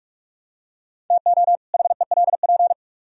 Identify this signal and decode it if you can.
Morse code, heard as a keyed tone, about 42 wpm, 700 Hz. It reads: TO HELP